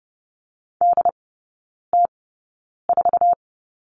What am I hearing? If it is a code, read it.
Morse code, heard as a keyed tone, about 30 wpm, spaced as for 9 wpm, 700 Hz. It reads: DT4
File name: Lesson 19e - Letter w - Three Characters 30wpm